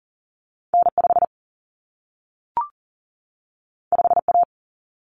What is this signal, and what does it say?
Morse code, heard as a keyed tone, about 40 wpm, 700 Hz. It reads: N5 5A
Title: Common Cut Numbers - 2-Digit 40wpm